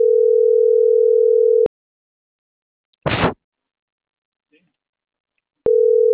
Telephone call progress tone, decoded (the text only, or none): ringback tone